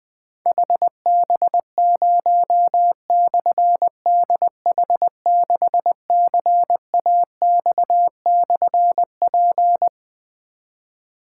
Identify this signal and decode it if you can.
Morse code, heard as a keyed tone, about 20 wpm, 700 Hz. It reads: HB0/DH6CAX/P
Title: Difficult Call Signs 20wpm